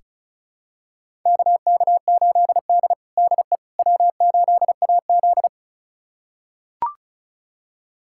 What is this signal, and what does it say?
Morse code, heard as a keyed tone, about 36 wpm, 700 Hz. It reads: KK8D DE W8AZ
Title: USA Callsign DE Callsign - 4 Characters - 35wpm